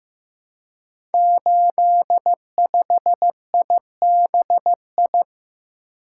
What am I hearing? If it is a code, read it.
Morse code, heard as a keyed tone, about 15 wpm, 700 Hz. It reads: 85IBI
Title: Code Groups - 5 Letter:Numbers - 15wpm